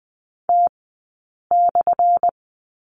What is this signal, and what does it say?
Morse code, heard as a keyed tone, about 20 wpm, 700 Hz. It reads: T /
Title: 177 Lesson 29d - Slash - Two Characters 20wpm